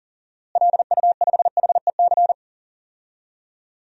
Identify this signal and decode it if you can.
Morse code, heard as a keyed tone, about 40 wpm, 700 Hz. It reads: LU5HEC
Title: DXCC-100 Call Signs - 6 Characters - Repeated 2x - 40wpm